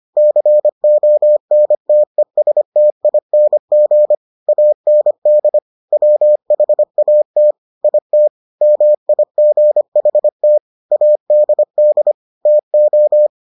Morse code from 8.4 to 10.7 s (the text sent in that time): MIGHT